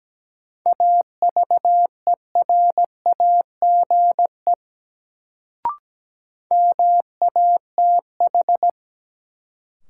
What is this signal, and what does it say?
Morse code, heard as a keyed tone, about 17 wpm, 700 Hz. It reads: AVERAGE MATH
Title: Sentences from Top 1000 Words - Review Part 6 - New Word Drill - 17wpm